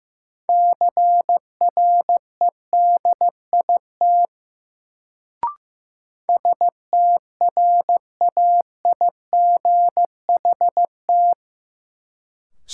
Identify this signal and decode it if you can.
Morse code, heard as a keyed tone, about 15 wpm, 700 Hz. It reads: CREDIT STRAIGHT
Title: Sentences from Top 1000 Words - New Word Drill - 15wpm